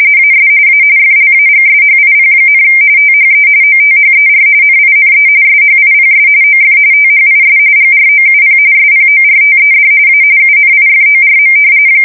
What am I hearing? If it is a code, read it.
RTTY, about 45.45 baud. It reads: AS A FLAGSHIP, THE OTHER NINE SHIPS OF THIS GROUP SPORTED A UNIFORM
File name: fsk170 / battleships